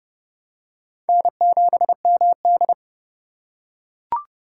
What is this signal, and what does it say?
Morse code, heard as a keyed tone, about 30 wpm, 700 Hz. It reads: N7MD